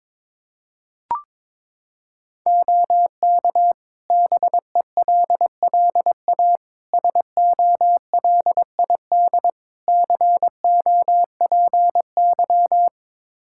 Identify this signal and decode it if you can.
Morse code, heard as a keyed tone, about 21 wpm, 700 Hz. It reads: OK BELLA SOLID COPY
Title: K1USN SST QSO Practice 22wpm